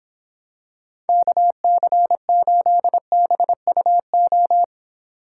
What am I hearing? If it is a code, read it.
Morse code, heard as a keyed tone, about 26 wpm, 700 Hz. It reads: KC8BUO